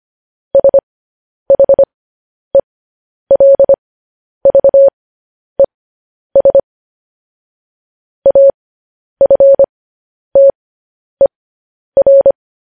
Morse code, heard as a keyed tone, about 25 wpm, spaced as for 10 wpm, 550 Hz. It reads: SHELVES AFTER
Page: news headlines in morse code at 10 wpm